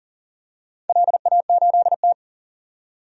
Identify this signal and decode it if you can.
Morse code, heard as a keyed tone, about 40 wpm, 700 Hz. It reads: LA8T
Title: DXCC-100 Call Signs - 4 Characters - 40wpm